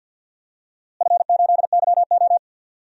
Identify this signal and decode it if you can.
Morse code, heard as a keyed tone, about 47 wpm, 700 Hz. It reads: F8YO
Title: DXCC-100 Call Signs - 4 Characters - Repeated 3x - 50wpm